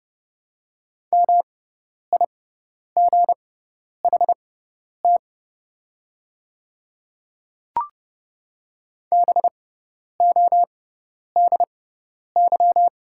Morse code, heard as a keyed tone, about 30 wpm, spaced as for 10 wpm, 700 Hz. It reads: MIGHT BODY
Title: 6x Character Spacing - Top 500 Words - 30wpm